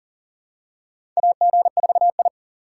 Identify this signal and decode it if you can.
Morse code, heard as a keyed tone, about 40 wpm, 700 Hz. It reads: AG4I
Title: USA Call Signs - 4 Characters - 2x Character Spacing - 40wpm